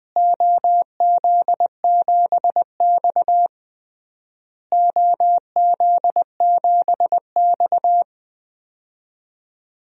Morse code, heard as a keyed tone, about 20 wpm, 700 Hz. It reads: OZ7X OZ7X
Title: DXCC-100 Call Signs - 4 Characters - Repeated 3x - 20wpm